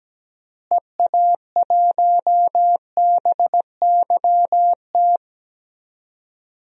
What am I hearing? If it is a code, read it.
Morse code, heard as a keyed tone, about 17 wpm, 700 Hz. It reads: EA1BYT
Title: DXCC-100 Call Signs - 6 Characters - Repeated 3x - 17wpm